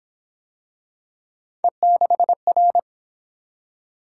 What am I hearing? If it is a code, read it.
Morse code, heard as a keyed tone, about 26 wpm, 700 Hz. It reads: E6R
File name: Speed-Racing - Three Letter-Number Combinations - 22wpm